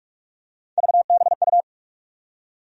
Morse code, heard as a keyed tone, about 43 wpm, 700 Hz. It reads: VBU
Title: USA Call Sign Postfixes - 3 Letters 45wpm